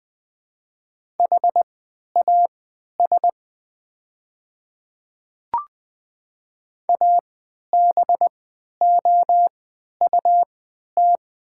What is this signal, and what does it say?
Morse code, heard as a keyed tone, about 20 wpm, spaced as for 11 wpm, 690 Hz. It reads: HAS ABOUT
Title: 3x Character Spacing - Top 100 Words - 20wpm